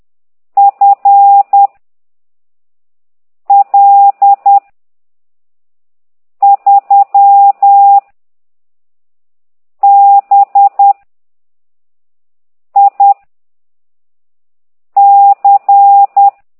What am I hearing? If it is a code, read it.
Morse code, heard as a keyed tone, about 10 wpm, 800 Hz. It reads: F L 3 B I C
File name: ct2idw / exerc13